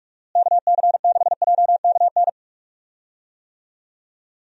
Morse code, heard as a keyed tone, about 46 wpm, 700 Hz. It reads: KC6JKN
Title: USA Call Signs - 6 Characters - Repeated 2x - 45wpm